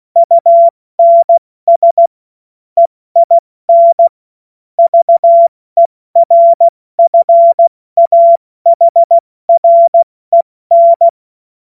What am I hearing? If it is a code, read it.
Morse code, heard as a keyed tone, about 16 wpm, spaced as for 14 wpm, 680 Hz. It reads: UNS EIN VERFAHREN
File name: Morsekurs / Witze / Witz 16. Dezember 2025 70 F80